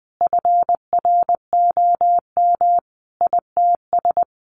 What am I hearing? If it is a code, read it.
Morse code, heard as a keyed tone, about 20 wpm, 700 Hz. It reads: FROM ITS
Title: A Sound Of Thunder 20wpm